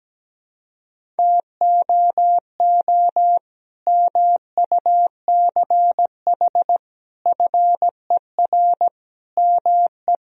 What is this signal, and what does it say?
Morse code, heard as a keyed tone, about 17 wpm, 700 Hz. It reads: TOO MUCH FER ME
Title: Sentences from Top 200 Words and Abbreviations 17wpm